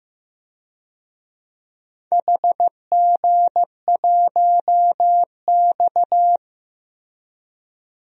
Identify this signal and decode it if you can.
Morse code, heard as a keyed tone, about 15 wpm, 700 Hz. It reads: HG1X
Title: DXCC-100 Call Signs - 4 Characters - Repeated 2x - 15wpm